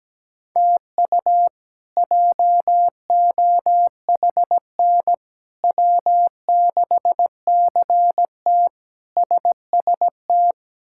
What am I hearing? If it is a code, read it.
Morse code, heard as a keyed tone, about 17 wpm, 700 Hz. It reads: TU JOHN W6CT SST
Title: K1USN SST QSO Practice - 17wpm